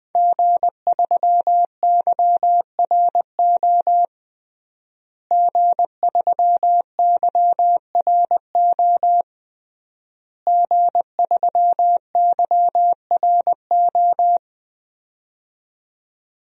Morse code, heard as a keyed tone, about 21 wpm, 700 Hz. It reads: G3YRO G3YRO G3YRO
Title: DXCC-100 Call Signs - 5 Characters - Repeated 3x - 20wpm